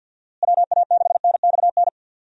Morse code, heard as a keyed tone, about 51 wpm, 700 Hz. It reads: WA6NXN
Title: USA Call Signs - 6 Characters - Repeated 2x - 50wpm